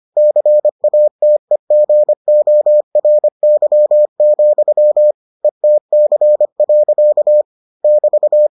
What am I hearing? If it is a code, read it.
Morse code, heard as a keyed tone, about 25 wpm, 600 Hz. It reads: CATEGORY, ETC. <BT>